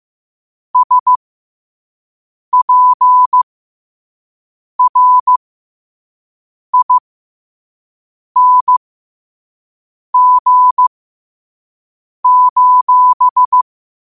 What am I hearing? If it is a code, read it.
Morse code, heard as a keyed tone, about 15 wpm, spaced as for 5 wpm, 1000 Hz. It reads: SPRING:
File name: 5wpm